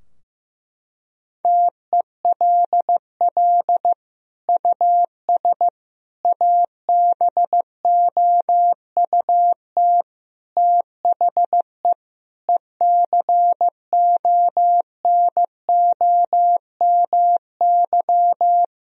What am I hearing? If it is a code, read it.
Morse code, heard as a keyed tone, about 15 wpm, 700 Hz. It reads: TELL US ABOUT THE ECONOMY
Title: Sentences from Top 1000 Words - Review Part 3 - 15wpm